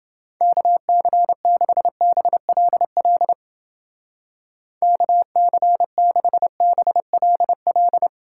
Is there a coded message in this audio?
Yes: Morse code from 0.4 to 8.1 s, heard as a keyed tone, about 28 wpm, 700 Hz. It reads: KC6BLL KC6BLL